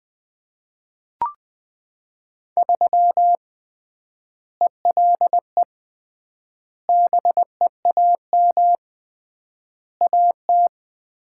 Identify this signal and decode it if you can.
Morse code, heard as a keyed tone, about 20 wpm, 700 Hz. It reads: 3 ELE BEAM AT
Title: Common QSO Phrases - 3x Word-Spacing - 20wpm